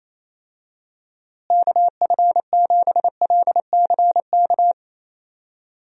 Morse code, heard as a keyed tone, about 28 wpm, 700 Hz. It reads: KF7LCK
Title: USA Call Signs - 6 Characters - Repeated 3x - 28wpm